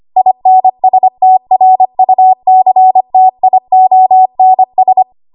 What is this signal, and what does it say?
Morse code, heard as a keyed tone, about 25 wpm, 750 Hz. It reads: INSTRUCTIONS
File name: Archive / 25 WPM / 25WPM